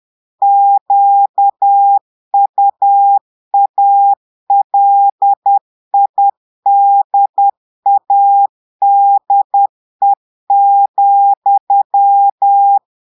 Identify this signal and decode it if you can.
Morse code, heard as a keyed tone, about 10 wpm, 800 Hz. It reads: QUALIDADE,
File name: portugais texte en clair 2 vitesse 10 mots minutes